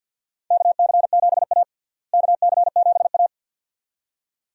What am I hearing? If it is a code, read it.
Morse code, heard as a keyed tone, about 52 wpm, 700 Hz. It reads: KC7A KC7A
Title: USA Call Signs - 4 Characters - Repeated 2x - 50wpm